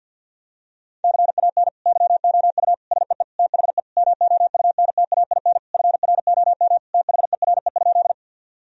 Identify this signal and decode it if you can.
Morse code, heard as a keyed tone, about 47 wpm, 700 Hz. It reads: CAN YOU SEE THE MOUNTAIN FROM THERE?
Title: Sentences from Top 1000 Words - Review Part 1 - 50wpm